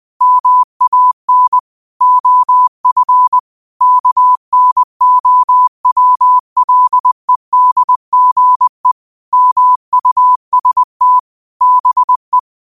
Morse code, heard as a keyed tone, about 20 wpm, 1000 Hz. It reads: MAN OF KNOWLEDGE MUST BE
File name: qotdfiles / 20wpm